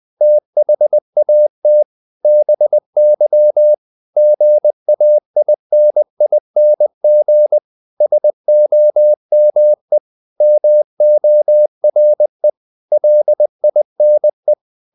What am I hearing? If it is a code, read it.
Morse code, heard as a keyed tone, about 20 wpm, 600 Hz. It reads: THAT BY GAINING SOME MORE LINE
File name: the old man and the sea part 2 20wpm